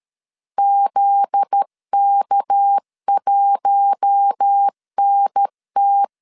Morse code, heard as a keyed tone, about 13 wpm, 800 Hz. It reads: ZK1NT
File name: LEZIONI / 7 GF9Z / 7 GF9Z 12wpm 2147